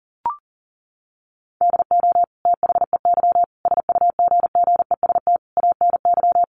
Morse code, heard as a keyed tone, about 40 wpm, 700 Hz. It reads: DO THEY SUGGEST ANY
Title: Sentences from Top 1300 Words 40wpm